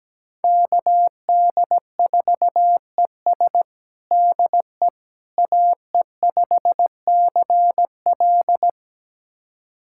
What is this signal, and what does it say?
Morse code, heard as a keyed tone, about 17 wpm, 700 Hz. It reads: KD4ES DE AE5CL
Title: USA Callsign DE Callsign - 5 Characters - 17wpm